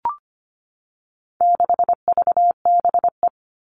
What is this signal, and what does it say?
Morse code, heard as a keyed tone, about 25 wpm, 700 Hz. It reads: 6VBE
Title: All Cut Numbers - 4-Digit 25wpm